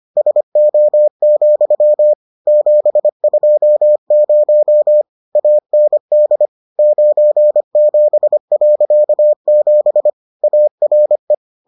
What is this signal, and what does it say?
Morse code, heard as a keyed tone, about 25 wpm, 600 Hz. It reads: SO, 720 AND 97.7 ARE